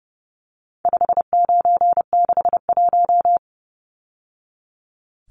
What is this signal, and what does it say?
Morse code, heard as a keyed tone, about 30 wpm, 700 Hz. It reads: 5961
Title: Numbers - 4-Digit 30wpm